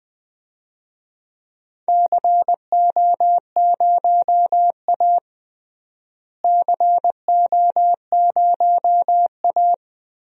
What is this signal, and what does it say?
Morse code, heard as a keyed tone, about 20 wpm, 700 Hz. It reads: CO0A CO0A
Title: DXCC-100 Call Signs - 4 Characters - Repeated 2x - 20wpm